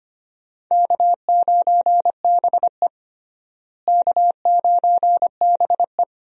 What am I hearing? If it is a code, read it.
Morse code, heard as a keyed tone, about 25 wpm, 700 Hz. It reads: K9BE K9BE
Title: USA Call Signs - 4 Characters - Repeated 3x - 25wpm